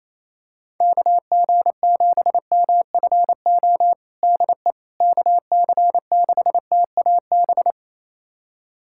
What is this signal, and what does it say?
Morse code, heard as a keyed tone, about 28 wpm, 700 Hz. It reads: KG7MFO DE KC6TAB
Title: USA Callsign DE Callsign 28wpm